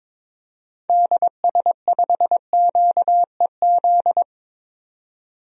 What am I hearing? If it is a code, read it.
Morse code, heard as a keyed tone, about 22 wpm, 700 Hz. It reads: DS5QEZ
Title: DXCC-100 Call Signs - 6 Characters - 22wpm